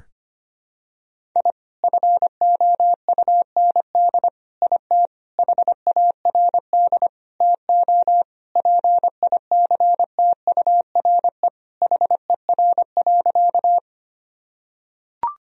Morse code, heard as a keyed tone, about 25 wpm, 700 Hz. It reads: I FOUND IT HARD TO PICTURE HER.
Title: Sentences from Top 1000 Words - Review Part 1 - 25wpm